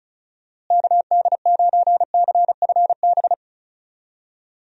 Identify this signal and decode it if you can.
Morse code, heard as a keyed tone, about 34 wpm, 700 Hz. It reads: KD9CFB
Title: USA Call Signs - 6 Characters - Repeated 2x - 35wpm